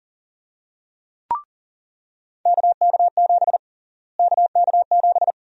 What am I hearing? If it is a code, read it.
Morse code, heard as a keyed tone, about 40 wpm, 700 Hz. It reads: KK7 KK7
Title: USA Call Sign Prefixes - Repeated 2x - 40wpm